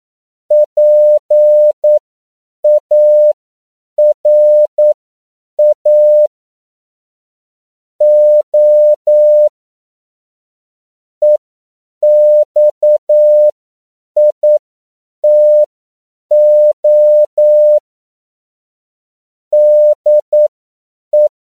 Morse code, heard as a keyed tone, about 9 wpm, spaced as for 7 wpm, 600 Hz. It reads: PARA O EXITO DE
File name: CW / 10PPM